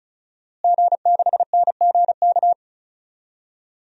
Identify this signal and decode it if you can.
Morse code, heard as a keyed tone, about 36 wpm, 700 Hz. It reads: G6NGK
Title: DXCC-100 Call Signs - 5 Characters - Repeated 3x - 35wpm